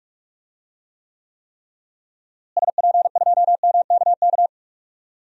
Morse code, heard as a keyed tone, about 46 wpm, 700 Hz. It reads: SP2MKK
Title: DXCC-100 Call Signs - 6 Characters - Repeated 3x - 45wpm